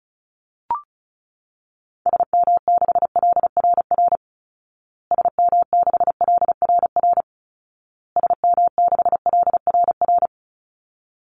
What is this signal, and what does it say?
Morse code, heard as a keyed tone, about 34 wpm, 700 Hz. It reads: SM6LRR SM6LRR SM6LRR
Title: Call Signs Repeated 2x - 35wpm